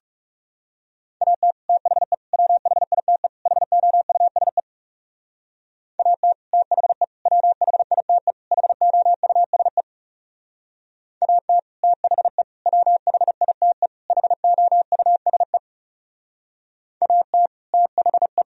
Morse code, heard as a keyed tone, about 37 wpm, 700 Hz. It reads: AT THE WHITE HOUSE AT THE WHITE HOUSE AT THE WHITE HOUSE AT THE
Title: Speed-Racing - Sets of 4 Words 30wpm